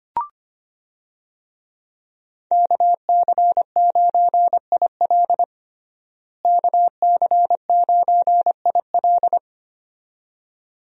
Morse code, heard as a keyed tone, about 25 wpm, 700 Hz. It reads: KC9IL KC9IL
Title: USA Call Signs - 5 Characters - Repeated 2x - 25wpm